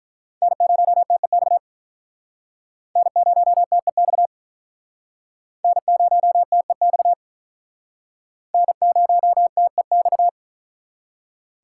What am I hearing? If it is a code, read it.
Morse code, heard as a keyed tone, about 45 wpm, 700 Hz. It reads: N0TEX N0TEX N0TEX N0TEX